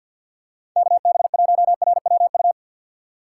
Morse code, heard as a keyed tone, about 53 wpm, 700 Hz. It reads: KB1RWU
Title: USA Call Signs - 4x Character Spacing - 50wpm